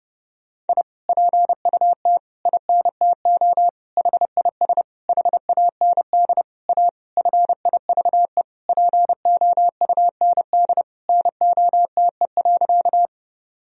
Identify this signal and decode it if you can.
Morse code, heard as a keyed tone, about 30 wpm, 700 Hz. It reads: I PUT INTO HIS HAND A FIVE POUND NOTE.
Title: Sentences from Top 1000 Words - Review Part 2 - 30wpm